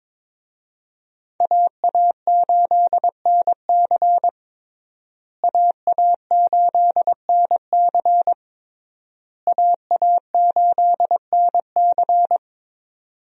Morse code, heard as a keyed tone, about 22 wpm, 700 Hz. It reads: AA8NC AA8NC AA8NC
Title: USA Call Signs - 5 Characters - Repeated 3x - 22wpm